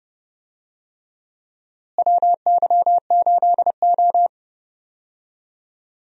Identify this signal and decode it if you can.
Morse code, heard as a keyed tone, about 30 wpm, 700 Hz. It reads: WY8O
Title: USA Call Signs - 4 Characters - Repeated 3x - 30wpm